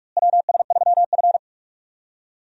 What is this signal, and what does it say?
Morse code, heard as a keyed tone, about 46 wpm, 700 Hz. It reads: WS3F
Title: USA Call Signs - 4 Characters - Repeated 2x - 45wpm